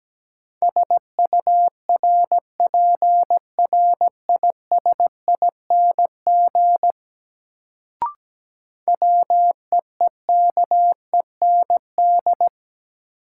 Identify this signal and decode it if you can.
Morse code, heard as a keyed tone, about 17 wpm, 700 Hz. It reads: SURPRISING WEEKEND
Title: Sets of 5 Words - New Word Drill - 17wpm